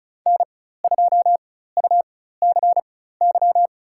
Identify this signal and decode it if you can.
Morse code, heard as a keyed tone, about 36 wpm, spaced as for 16 wpm, 700 Hz. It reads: N2UCY
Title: USA Call Signs - 5 Characters - 4x Character Spacing - 35wpm